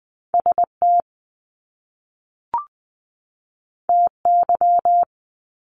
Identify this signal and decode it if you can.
Morse code, heard as a keyed tone, about 20 wpm, 700 Hz. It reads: ST TY
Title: Top Two Letter Combinations in Words 20wpm